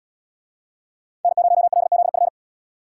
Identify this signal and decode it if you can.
Morse code, heard as a keyed tone, about 77 wpm, 700 Hz. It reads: N0LZF